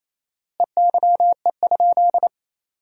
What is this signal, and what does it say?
Morse code, heard as a keyed tone, about 27 wpm, 700 Hz. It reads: EYE?
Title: 4x Word-Spacing - Sentences from Top 300 Words - 28wpm